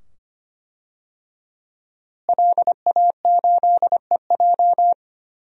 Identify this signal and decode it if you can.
Morse code, heard as a keyed tone, about 25 wpm, 700 Hz. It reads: LA8EJ
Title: DXCC-100 Call Signs - 5 Characters - Repeated 2x - 25wpm